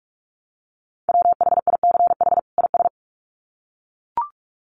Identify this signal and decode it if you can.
Morse code, heard as a keyed tone, about 44 wpm, 700 Hz. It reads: WHICH IS
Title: Sets of 2 Words 45wpm